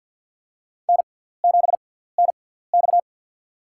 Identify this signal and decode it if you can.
Morse code, heard as a keyed tone, about 48 wpm, spaced as for 17 wpm, 700 Hz. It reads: N7NX
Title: USA Call Signs - 4 Characters - 6x Character Spacing - 50wpm